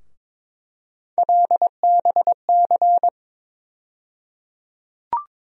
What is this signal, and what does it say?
Morse code, heard as a keyed tone, about 22 wpm, 700 Hz. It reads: LBC
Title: USA Call Sign Postfixes - 3 Letters 22wpm